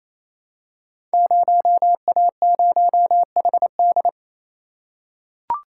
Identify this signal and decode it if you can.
Morse code, heard as a keyed tone, about 28 wpm, 700 Hz. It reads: 0A0HD
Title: Code Groups - 5 Letter:Numbers - 28wpm